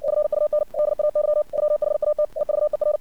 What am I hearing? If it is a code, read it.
Morse code, heard as a keyed tone, about 57 wpm, 600 Hz. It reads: FUN DAY WHAT EVER
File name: archived / CRSnet